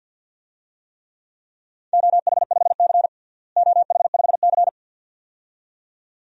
Morse code, heard as a keyed tone, about 52 wpm, 700 Hz. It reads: OH5C OH5C
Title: DXCC-100 Call Signs - 4 Characters - Repeated 2x - 50wpm